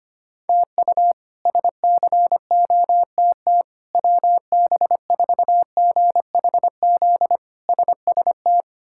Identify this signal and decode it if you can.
Morse code, heard as a keyed tone, about 25 wpm, 700 Hz. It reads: TU SCOTT WB4GHZ SST
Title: K1USN SST QSO Practice - 25wpm